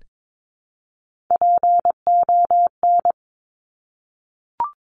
Morse code, heard as a keyed tone, about 22 wpm, 700 Hz. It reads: PON
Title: Top Three Letter Combinations in Words 22wpm